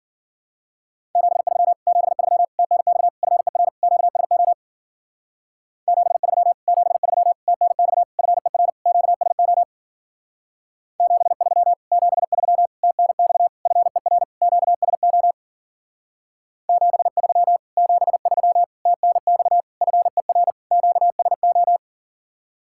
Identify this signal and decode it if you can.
Morse code, heard as a keyed tone, about 48 wpm, 700 Hz. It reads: 73 73 TNX FER QSO 73 73 TNX FER QSO 73 73 TNX FER QSO 73 73 TNX FER QSO
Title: Speed-Racing Common QSO Phrases Encore - 40wpm